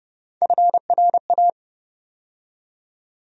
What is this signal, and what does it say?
Morse code, heard as a keyed tone, about 30 wpm, 700 Hz. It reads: FRA
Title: USA Call Sign Postfixes - 3 Letters 30wpm